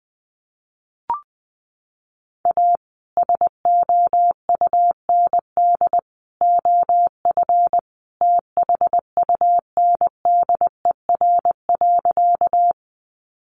Morse code, heard as a keyed tone, about 20 wpm, 700 Hz. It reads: A SOUND OF THUNDER.
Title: A Sound Of Thunder 20wpm